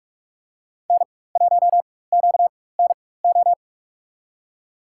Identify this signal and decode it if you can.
Morse code, heard as a keyed tone, about 43 wpm, spaced as for 21 wpm, 690 Hz. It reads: N1QNO
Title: USA Call Signs - 5 Characters - 4x Character Spacing - 45wpm